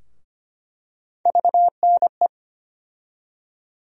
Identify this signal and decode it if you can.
Morse code, heard as a keyed tone, about 25 wpm, 700 Hz. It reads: VNE
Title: USA Call Sign Postfixes - 3 Letters 25wpm